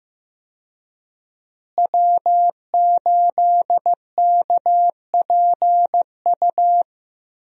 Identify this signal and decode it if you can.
Morse code, heard as a keyed tone, about 15 wpm, 700 Hz. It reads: W8KPU
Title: USA Call Signs - 5 Characters - Repeated 2x - 15wpm